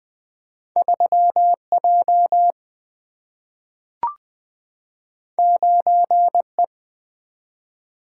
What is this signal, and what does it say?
Morse code, heard as a keyed tone, about 20 wpm, 700 Hz. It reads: 3J 9E
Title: Single-Letter Single-Number 20wpm